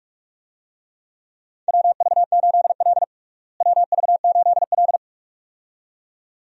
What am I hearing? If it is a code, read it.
Morse code, heard as a keyed tone, about 46 wpm, 700 Hz. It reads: WV8L WV8L
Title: USA Call Signs - 4 Characters - Repeated 2x - 45wpm